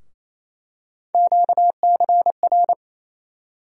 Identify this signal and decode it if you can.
Morse code, heard as a keyed tone, about 28 wpm, 700 Hz. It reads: QCR
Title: USA Call Sign Postfixes - 3 Letters 28wpm